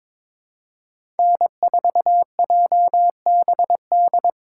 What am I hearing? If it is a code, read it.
Morse code, heard as a keyed tone, about 22 wpm, 700 Hz. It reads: N4JBD
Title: USA Call Signs - 5 Characters - 2x Character Spacing - 22wpm